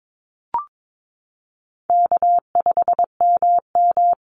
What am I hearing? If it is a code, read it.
Morse code, heard as a keyed tone, about 22 wpm, 700 Hz. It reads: K5MM
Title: Call Signs Repeated 2x - 22wpm